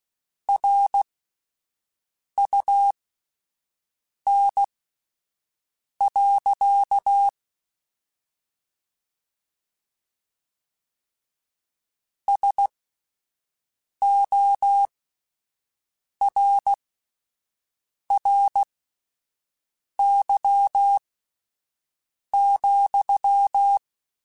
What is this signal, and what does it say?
Morse code, heard as a keyed tone, about 16 wpm, spaced as for 5 wpm, 800 Hz. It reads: RUN. SORRY,
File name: QSOs Spaced